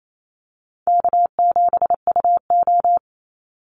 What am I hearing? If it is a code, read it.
Morse code, heard as a keyed tone, about 28 wpm, 700 Hz. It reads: K7UO
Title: USA Call Signs 28wpm